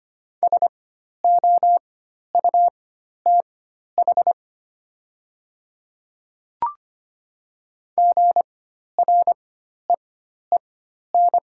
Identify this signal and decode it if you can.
Morse code, heard as a keyed tone, about 25 wpm, spaced as for 12 wpm, 700 Hz. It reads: SOUTH GREEN
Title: 4x Character Spacing - Top 500 Words - 25wpm